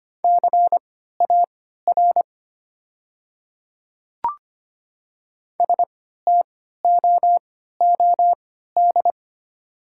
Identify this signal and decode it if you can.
Morse code, heard as a keyed tone, about 25 wpm, spaced as for 14 wpm, 700 Hz. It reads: CAR STOOD
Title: 3x Character Spacing - Top 500 Words - 25wpm